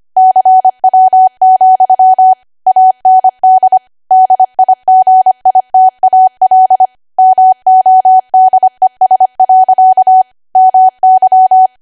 Morse code, heard as a keyed tone, about 25 wpm, 750 Hz. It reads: CW, AND DIGITAL MODES. MY
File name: Archive / 25 WPM / 25WPM